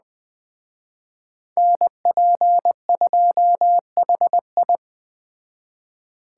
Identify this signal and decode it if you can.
Morse code, heard as a keyed tone, about 20 wpm, 700 Hz. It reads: NP2HI